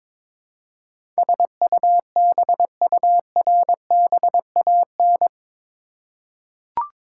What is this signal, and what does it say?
Morse code, heard as a keyed tone, about 22 wpm, 700 Hz. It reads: SUBURBAN
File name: practice / 22wpm / Eight Letter Words - 22wpm